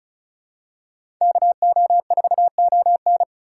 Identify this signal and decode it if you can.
Morse code, heard as a keyed tone, about 36 wpm, 700 Hz. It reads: KO4ON
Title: USA Call Signs - 3x Character Spacing - 35wpm